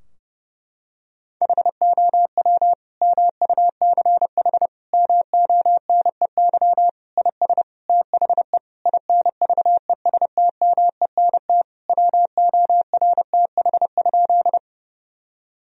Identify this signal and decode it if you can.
Morse code, heard as a keyed tone, about 32 wpm, 700 Hz. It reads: HOW MUCH MONEY IS THE INVESTMENT WORTH?
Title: Sentences from Top 1000 Words - Review Part 4 - 30wpm